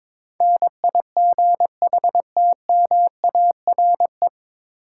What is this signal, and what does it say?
Morse code, heard as a keyed tone, about 22 wpm, 700 Hz. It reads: NIGHTMARE
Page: Morse Code Ninja practice 22wpm